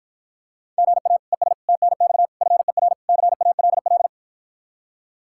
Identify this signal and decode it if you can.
Morse code, heard as a keyed tone, about 53 wpm, 700 Hz. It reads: GA ES TNX FER CALL